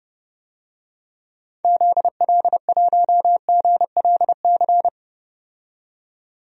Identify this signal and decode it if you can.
Morse code, heard as a keyed tone, about 30 wpm, 700 Hz. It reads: ZL1GLC
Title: DXCC-100 Call Signs - 6 Characters - 30wpm